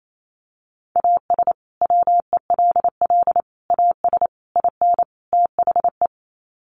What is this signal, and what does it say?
Morse code, heard as a keyed tone, about 28 wpm, 700 Hz. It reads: AS WELL AS IN THE
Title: Sets of 5 Words Encore 28wpm